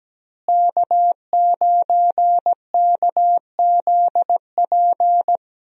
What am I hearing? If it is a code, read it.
Morse code, heard as a keyed tone, about 17 wpm, 700 Hz. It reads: K9KZP
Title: USA Call Signs - 5 Characters - Repeated 3x - 17wpm